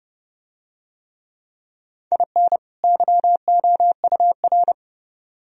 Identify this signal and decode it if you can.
Morse code, heard as a keyed tone, about 30 wpm, 700 Hz. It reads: IN YOUR